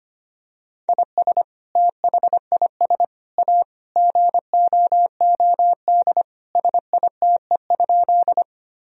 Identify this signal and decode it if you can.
Morse code, heard as a keyed tone, about 25 wpm, 700 Hz. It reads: IS THIS A GOOD SITE?
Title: Sentences from Top 1000 Words - Review Part 4 - 25wpm